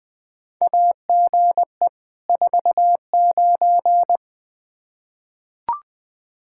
Morse code, heard as a keyed tone, about 20 wpm, 700 Hz. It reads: AGE 49
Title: Practice QSO - Exchange 3 - More Info - 20wpm